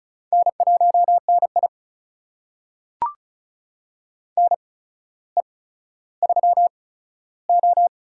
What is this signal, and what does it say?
Morse code, heard as a keyed tone, about 34 wpm, spaced as for 15 wpm, 700 Hz. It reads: N1NI NE3O
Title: USA Call Signs - 4 Characters - 8x Character Spacing - 35wpm